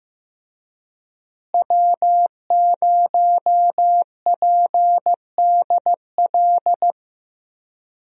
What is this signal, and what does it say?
Morse code, heard as a keyed tone, about 15 wpm, 700 Hz. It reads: W0PDL
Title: USA Call Signs - 5 Characters - Repeated 3x - 15wpm